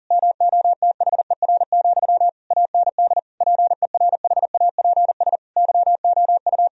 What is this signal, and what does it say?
Morse code, heard as a keyed tone, about 37 wpm, 700 Hz. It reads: MOTHER, AND PERHAPS YOU
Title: Sentences from Top 1000 Words - Review Part 1 - 40wpm